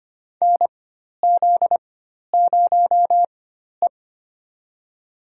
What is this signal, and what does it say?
Morse code, heard as a keyed tone, about 25 wpm, spaced as for 12 wpm, 700 Hz. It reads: NZ0E
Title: USA Call Signs - 4 Characters - 4x Character Spacing - 25wpm